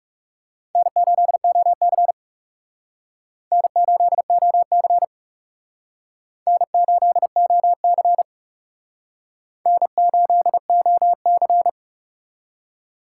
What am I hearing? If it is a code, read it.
Morse code, heard as a keyed tone, about 39 wpm, 700 Hz. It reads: N8OC N8OC N8OC N8OC